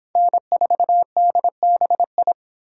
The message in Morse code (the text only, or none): N4DBI